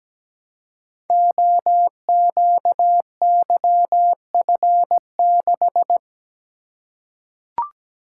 Morse code, heard as a keyed tone, about 17 wpm, 700 Hz. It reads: OQYF6
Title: Code Groups - 5 Letter:Numbers - 17wpm